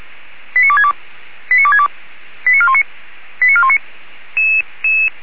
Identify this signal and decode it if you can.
Five-tone selective call (ZVEI), heard as quick runs of five tones, tones of about 70 ms. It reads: 89262 89262 89419 89419